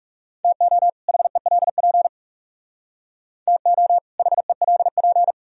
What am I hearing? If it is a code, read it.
Morse code, heard as a keyed tone, about 35 wpm, 700 Hz. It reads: TO HELP TO HELP
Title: Speed-Racing - Sets of 2 Words 30wpm